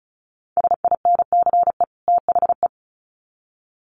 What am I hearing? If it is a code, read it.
Morse code, heard as a keyed tone, about 34 wpm, 700 Hz. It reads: SINCE THE
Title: Sets of 2 Words 35wpm